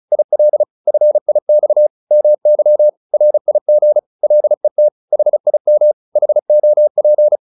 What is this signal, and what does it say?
Morse code, heard as a keyed tone, about 34 wpm, 600 Hz. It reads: IL FIX MY RIG LET HIM HOP